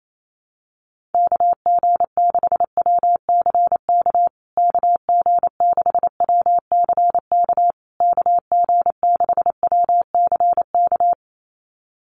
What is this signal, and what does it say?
Morse code, heard as a keyed tone, about 28 wpm, 700 Hz. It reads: KG6WCK KG6WCK KG6WCK
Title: USA Call Signs Repeated 3x 28wpm